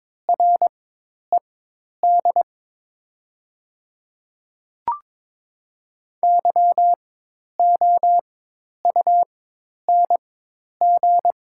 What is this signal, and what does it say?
Morse code, heard as a keyed tone, about 22 wpm, spaced as for 10 wpm, 700 Hz. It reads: RED YOUNG
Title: 4x Character Spacing - Top 500 Words - 22wpm